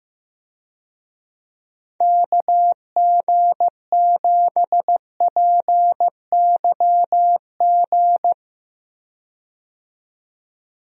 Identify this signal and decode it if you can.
Morse code, heard as a keyed tone, about 15 wpm, 700 Hz. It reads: KG7PYG
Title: USA Call Signs - 6 Characters - 15wpm